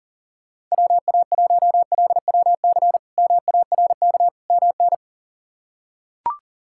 Morse code, heard as a keyed tone, about 37 wpm, 690 Hz. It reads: WA1LWC MARK MN
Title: K1USN SST QSO Practice - 40wpm